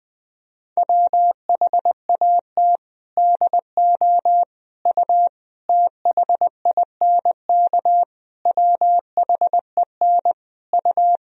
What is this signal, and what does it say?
Morse code, heard as a keyed tone, about 21 wpm, 700 Hz. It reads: WHAT DO U THINK WHEN U
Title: Sentences from Top 300 Words and Abbreviations 20wpm